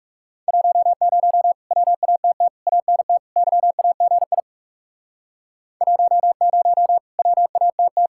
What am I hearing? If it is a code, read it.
Morse code, heard as a keyed tone, about 40 wpm, 700 Hz. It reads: 10 WATT ANT YAGI 10 WATT